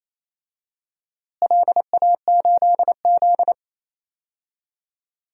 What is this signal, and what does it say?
Morse code, heard as a keyed tone, about 28 wpm, 700 Hz. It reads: LA8Z
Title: DXCC-100 Call Signs - 4 Characters - Repeated 2x - 28wpm